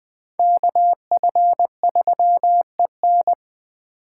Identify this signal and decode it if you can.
Morse code, heard as a keyed tone, about 19 wpm, 700 Hz. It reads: KF3EN